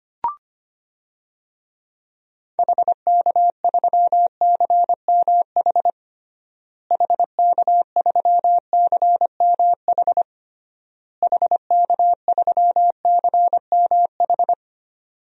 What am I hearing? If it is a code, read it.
Morse code, heard as a keyed tone, about 25 wpm, 700 Hz. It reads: HK3CMH HK3CMH HK3CMH
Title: DXCC-100 Call Signs - 6 Characters - Repeated 3x - 25wpm